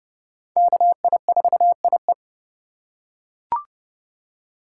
Morse code, heard as a keyed tone, about 32 wpm, 700 Hz. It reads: KI4IE
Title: USA Call Signs - 5 Characters - 2x Character Spacing - 30wpm